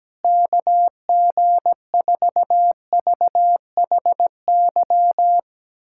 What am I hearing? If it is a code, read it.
Morse code, heard as a keyed tone, about 17 wpm, 700 Hz. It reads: KG4VHY
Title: USA Callsign DE Callsign 17wpm